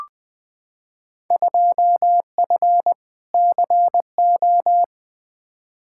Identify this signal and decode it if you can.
Morse code, heard as a keyed tone, about 20 wpm, 700 Hz. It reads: 2F CO
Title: Field Day Exchanges - 20wpm